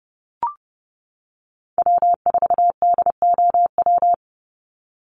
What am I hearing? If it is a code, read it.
Morse code, heard as a keyed tone, about 30 wpm, 700 Hz. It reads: W4DOW
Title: NA SOTA Chaser Callsigns 30wpm